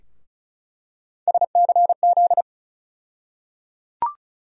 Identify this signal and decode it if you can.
Morse code, heard as a keyed tone, about 34 wpm, 700 Hz. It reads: SCZ
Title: USA Call Sign Postfixes - 3 Letters 35wpm